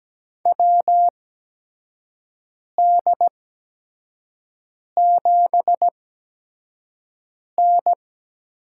Morse code, heard as a keyed tone, about 17 wpm, spaced as for 5 wpm, 700 Hz. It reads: WD7N
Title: USA Call Signs - 4 Characters - 8x Character Spacing - 17wpm